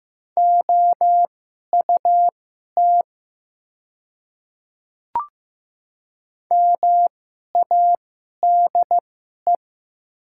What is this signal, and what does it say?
Morse code, heard as a keyed tone, about 15 wpm, spaced as for 11 wpm, 700 Hz. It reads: OUT MADE
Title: 2x Character Spacing - Top 200 Words - 15wpm